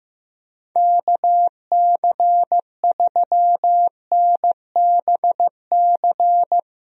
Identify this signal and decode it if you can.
Morse code, heard as a keyed tone, about 15 wpm, 700 Hz. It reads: KC3NBC
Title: USA Call Signs - 6 Characters - Repeated 2x - 15wpm